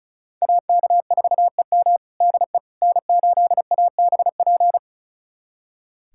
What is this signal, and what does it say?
Morse code, heard as a keyed tone, about 36 wpm, 700 Hz. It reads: AK4EM DE N8ABP
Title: USA Callsign DE Callsign - 5 Characters - 35wpm